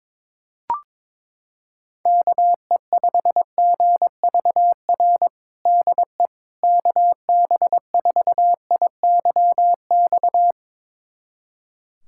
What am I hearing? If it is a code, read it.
Morse code, heard as a keyed tone, about 22 wpm, 700 Hz. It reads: KE5GVR DE KB4IYX
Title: USA Callsign DE Callsign 22wpm